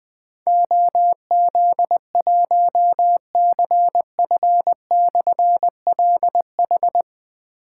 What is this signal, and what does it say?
Morse code, heard as a keyed tone, about 20 wpm, 700 Hz. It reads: OZ1CF/LH
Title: Difficult Call Signs 20wpm